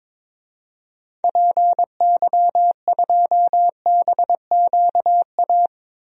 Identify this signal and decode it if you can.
Morse code, heard as a keyed tone, about 22 wpm, 700 Hz. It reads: PY2BQA
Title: DXCC-100 Call Signs - 6 Characters - Repeated 3x - 22wpm